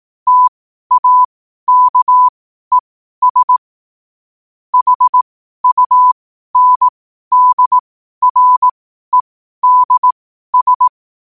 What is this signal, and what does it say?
Morse code, heard as a keyed tone, about 18 wpm, spaced as for 12 wpm, 1000 Hz. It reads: TAKES HUNDREDS
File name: qotdfiles / 12wpm